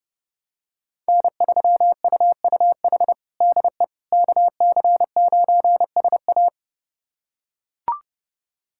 Morse code, heard as a keyed tone, about 30 wpm, 700 Hz. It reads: N3UUH DE KC9SA
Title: USA Callsign DE Callsign - 5 Characters - 30wpm